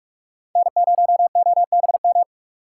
Morse code, heard as a keyed tone, about 39 wpm, 700 Hz. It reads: N0OBM